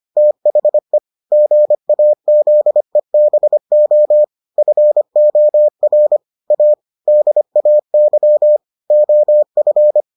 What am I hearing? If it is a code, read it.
Morse code, heard as a keyed tone, about 25 wpm, 600 Hz. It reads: THE GAZEBO FOR A DAY OF